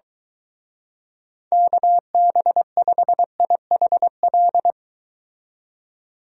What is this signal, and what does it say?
Morse code, heard as a keyed tone, about 24 wpm, 700 Hz. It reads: KB5IHL